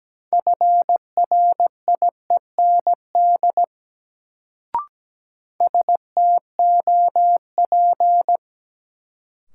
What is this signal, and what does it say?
Morse code, heard as a keyed tone, about 17 wpm, 700 Hz. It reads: FRIEND STOP
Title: Sentences from Top 300 Words - New Word Drill - 17wpm